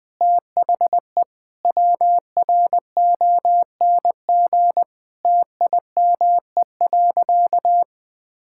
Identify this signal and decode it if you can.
Morse code, heard as a keyed tone, about 20 wpm, 700 Hz. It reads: THE WRONG TIME.
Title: Sentences with 8-Word Phrases - 20wpm